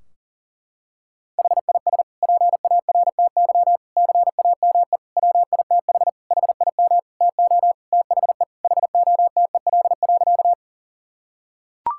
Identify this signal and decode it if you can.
Morse code, heard as a keyed tone, about 40 wpm, 700 Hz. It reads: HIS PARTY CAME WITH HIM TO THE HOTEL.
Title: Sentences from Top 1600 Words 40wpm